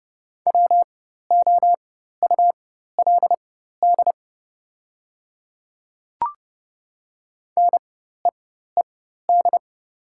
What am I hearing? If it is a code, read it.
Morse code, heard as a keyed tone, about 30 wpm, spaced as for 14 wpm, 700 Hz. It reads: WOULD NEED
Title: 4x Character Spacing - Top 200 Words - 30wpm